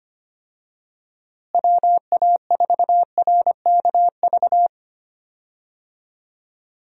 Morse code, heard as a keyed tone, about 25 wpm, 700 Hz. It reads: WA4RKV
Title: USA Call Signs - 6 Characters - Repeated 3x - 25wpm